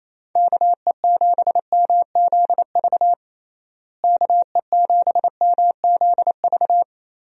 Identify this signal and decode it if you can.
Morse code, heard as a keyed tone, about 27 wpm, 700 Hz. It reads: KE7MZV KE7MZV
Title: USA Call Signs - 6 Characters - Repeated 2x - 28wpm